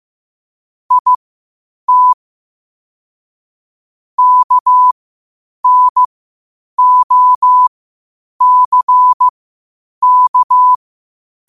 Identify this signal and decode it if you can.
Morse code, heard as a keyed tone, about 15 wpm, spaced as for 8 wpm, 1000 Hz. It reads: IT KNOCK
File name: qotdfiles / 8wpm